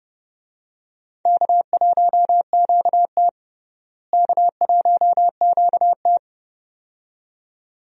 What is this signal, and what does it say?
Morse code, heard as a keyed tone, about 30 wpm, 700 Hz. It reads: K1QT K1QT
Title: USA Call Signs - 4 Characters - Repeated 2x - 30wpm